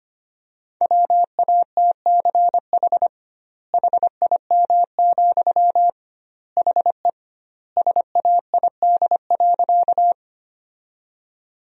Morse code, heard as a keyed tone, about 25 wpm, 700 Hz. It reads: WATCH HIM, HE SAID.
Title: 2x Word-Spacing - Sentences from Top 300 Words - 25wpm